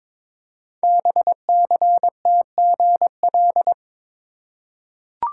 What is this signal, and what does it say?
Morse code, heard as a keyed tone, about 22 wpm, 700 Hz. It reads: BCTGL
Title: Code Groups - 5 Letters - 22wpm